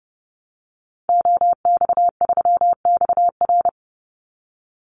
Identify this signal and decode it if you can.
Morse code, heard as a keyed tone, about 30 wpm, 700 Hz. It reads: OX3XR
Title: Call Signs Repeated 3x - 30wpm